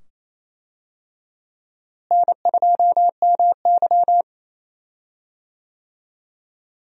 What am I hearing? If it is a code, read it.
Morse code, heard as a keyed tone, about 28 wpm, 700 Hz. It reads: N2MY
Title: USA Call Signs - 4 Characters - Repeated 2x - 28wpm